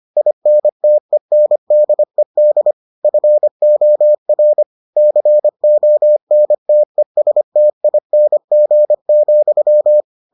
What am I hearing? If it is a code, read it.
Morse code, heard as a keyed tone, about 25 wpm, 600 Hz. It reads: INTENDED FOR CONTESTING,